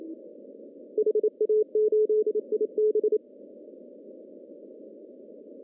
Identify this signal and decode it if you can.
Morse code, heard as a keyed tone, about 28 wpm, 430 Hz. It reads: HA8IB
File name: CW / cqww-cw-k5tr